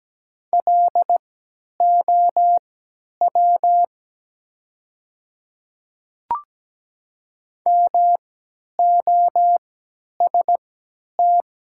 Morse code, heard as a keyed tone, about 17 wpm, spaced as for 10 wpm, 700 Hz. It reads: LOW MOST
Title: 3x Character Spacing - Top 200 Words - 17wpm